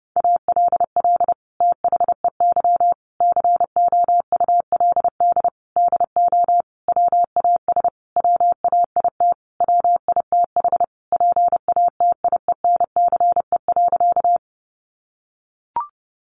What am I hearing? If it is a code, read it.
Morse code, heard as a keyed tone, about 30 wpm, 700 Hz. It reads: ALL THEY COULD DO WAS WAIT WITH PATIENCE.
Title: Sentences from Top 1300 Words 30wpm